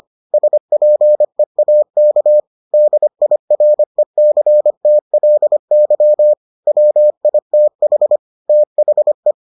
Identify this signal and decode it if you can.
Morse code, heard as a keyed tone, about 25 wpm, 600 Hz. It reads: SPEAK DIRECTLY WITH THE